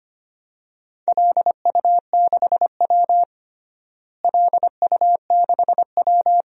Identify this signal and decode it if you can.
Morse code, heard as a keyed tone, about 25 wpm, 700 Hz. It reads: LU6W LU6W
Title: DXCC-100 Call Signs - 4 Characters - Repeated 2x - 25wpm